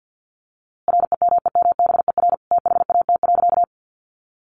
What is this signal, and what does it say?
Morse code, heard as a keyed tone, about 50 wpm, 700 Hz. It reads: REMEMBER THAT.
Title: A Sound Of Thunder 50wpm